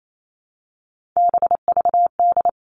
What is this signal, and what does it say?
Morse code, heard as a keyed tone, about 28 wpm, 700 Hz. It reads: BVD
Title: Three Challenging Letter-Number Combinations 28wpm